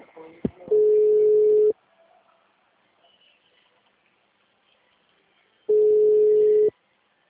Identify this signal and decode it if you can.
Ringback tone (Europe, 425 Hz), with speech.
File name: out- -2009- - - .28887